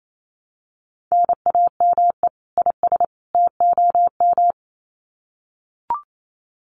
Morse code, heard as a keyed tone, about 28 wpm, 700 Hz. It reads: NAME IS TOM
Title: Common QSO Phrases 28wpm